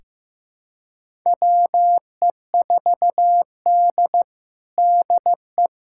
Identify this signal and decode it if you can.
Morse code, heard as a keyed tone, about 15 wpm, 700 Hz. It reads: WE4D DE
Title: USA Callsign DE Callsign - 4 Characters - 15wpm